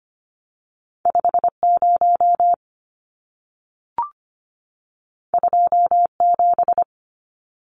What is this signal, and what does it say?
Morse code, heard as a keyed tone, about 25 wpm, 700 Hz. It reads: 50 27
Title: Numbers - 2-Digit 25wpm